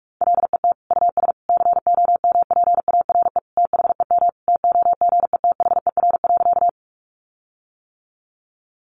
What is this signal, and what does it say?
Morse code, heard as a keyed tone, about 44 wpm, 700 Hz. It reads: LET US COMPARE THEM TOGETHER.
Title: Sentences from Top 1300 Words 45wpm